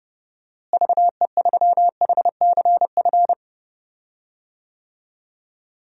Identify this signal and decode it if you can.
Morse code, heard as a keyed tone, about 30 wpm, 700 Hz. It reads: VE3HCF
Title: DXCC-100 Call Signs - 6 Characters - Repeated 3x - 30wpm